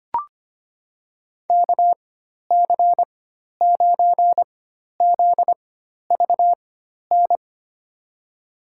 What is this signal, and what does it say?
Morse code, heard as a keyed tone, about 25 wpm, spaced as for 12 wpm, 700 Hz. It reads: KC9ZVN
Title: USA Call Signs - 6 Characters - 4x Character Spacing - 25wpm